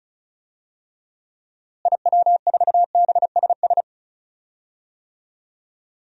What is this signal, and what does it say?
Morse code, heard as a keyed tone, about 34 wpm, 700 Hz. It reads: IW4BSS
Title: DXCC-100 Call Signs - 6 Characters - Repeated 3x - 35wpm